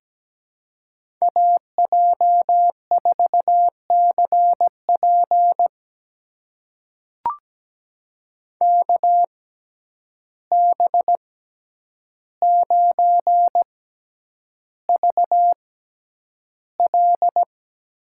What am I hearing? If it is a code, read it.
Morse code, heard as a keyed tone, about 17 wpm, spaced as for 9 wpm, 700 Hz. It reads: AJ4CP KB9VL
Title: USA Call Signs - 6x Character Spacing - 17wpm